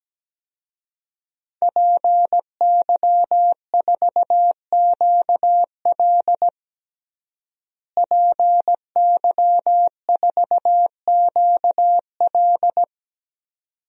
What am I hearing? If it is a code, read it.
Morse code, heard as a keyed tone, about 17 wpm, 700 Hz. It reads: PY4QL PY4QL
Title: DXCC-100 Call Signs - 5 Characters - Repeated 2x - 17wpm